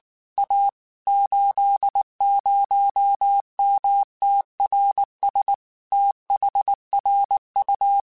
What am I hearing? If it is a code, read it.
Morse code, heard as a keyed tone, about 19 wpm, 800 Hz. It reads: A 80MTRS THRU
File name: QSOs / QSOs-mp3s / WD0FBT 1@18